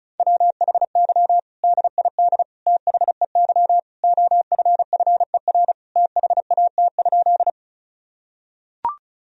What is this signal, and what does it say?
Morse code, heard as a keyed tone, about 36 wpm, 700 Hz. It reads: WHY DID THEY OFFER THAT?
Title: Sentences from Top 1000 Words - Review Part 4 - 35wpm